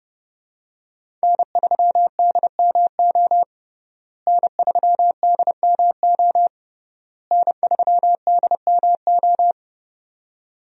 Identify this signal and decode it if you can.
Morse code, heard as a keyed tone, about 30 wpm, 700 Hz. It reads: N3DMO N3DMO N3DMO
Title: USA Call Signs - 5 Characters - Repeated 3x - 30wpm